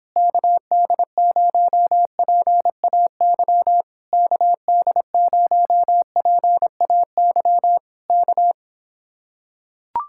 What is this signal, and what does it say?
Morse code, heard as a keyed tone, about 26 wpm, 700 Hz. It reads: KD0PAY KD0PAY K